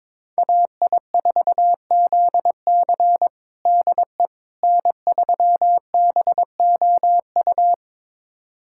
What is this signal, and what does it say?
Morse code, heard as a keyed tone, about 22 wpm, 700 Hz. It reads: AI4ZC DE N3BOU
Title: USA Callsign DE Callsign - 5 Characters - 22wpm